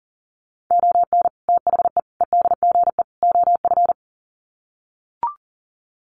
Morse code, heard as a keyed tone, about 40 wpm, 700 Hz. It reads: ON THE EDGE OF
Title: Sets of 4 Words 40wpm